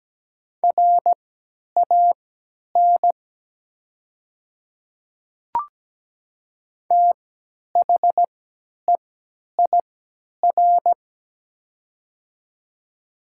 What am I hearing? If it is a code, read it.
Morse code, heard as a keyed tone, about 17 wpm, spaced as for 10 wpm, 690 Hz. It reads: RAN THEIR
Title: 3x Character Spacing - Top 500 Words - 17wpm